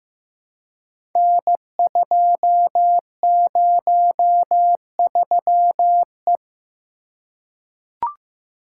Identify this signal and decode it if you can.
Morse code, heard as a keyed tone, about 15 wpm, 700 Hz. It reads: N203E
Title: Code Groups - 5 Letter:Numbers - 15wpm